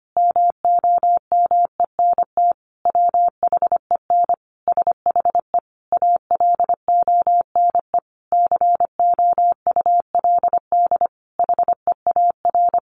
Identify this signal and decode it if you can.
Morse code, heard as a keyed tone, about 25 wpm, 700 Hz. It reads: MOMENT WHEN SHE ALONE COULD HEAR
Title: Sentences from Top 1000 Words 25wpm